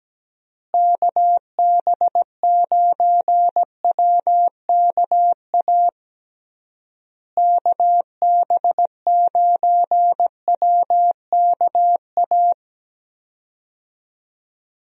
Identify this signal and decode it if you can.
Morse code, heard as a keyed tone, about 17 wpm, 700 Hz. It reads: KB9WKA KB9WKA
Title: USA Call Signs - 6 Characters - Repeated 3x - 17wpm